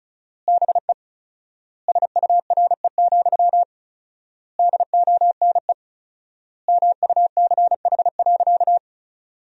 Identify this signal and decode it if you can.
Morse code, heard as a keyed tone, about 34 wpm, 700 Hz. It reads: BE SURE, DONE MUCH.
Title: 4x Word-Spacing - Sentences from Top 500 Words - 35wpm